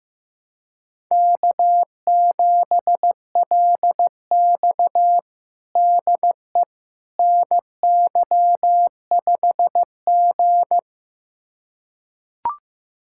Morse code, heard as a keyed tone, about 15 wpm, 700 Hz. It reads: K7LX DE NY5G
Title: USA Callsign DE Callsign - 4 Characters - 15wpm